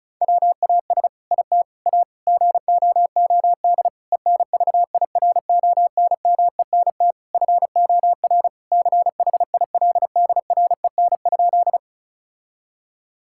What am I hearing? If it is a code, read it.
Morse code, heard as a keyed tone, about 36 wpm, 700 Hz. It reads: WAS IT A GOOD ENVIRONMENT FOR CHILDREN?
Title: Sentences from Top 1000 Words - Review Part 5 - 35wpm